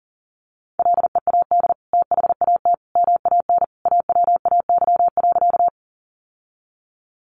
Morse code, heard as a keyed tone, about 40 wpm, 700 Hz. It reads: LEAD THAT MAN AWAY.
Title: Sentences from Top 600 Words 40wpm